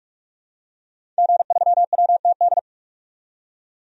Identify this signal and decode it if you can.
Morse code, heard as a keyed tone, about 46 wpm, 700 Hz. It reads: G3WTD
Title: DXCC-100 Call Signs - 5 Characters - 45wpm